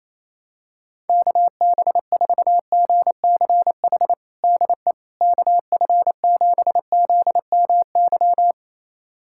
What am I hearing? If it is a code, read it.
Morse code, heard as a keyed tone, about 28 wpm, 700 Hz. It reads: KB4GCH DE KF7ZMY
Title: USA Callsign DE Callsign 28wpm